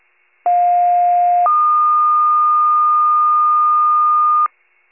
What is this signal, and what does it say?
Two-tone page, heard as a one-second tone, then a three-second tone, about 710 then 1220 Hz. It is heard from 0.5 to 4.5 s.
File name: pager / 95 RESCUE DUTY / wcalert